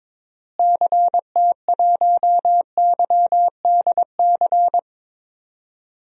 Morse code, heard as a keyed tone, about 22 wpm, 700 Hz. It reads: CT1YDC
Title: DXCC-100 Call Signs - 6 Characters - Repeated 3x - 22wpm